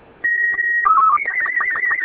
SSTV (slow-scan television), Robot 36.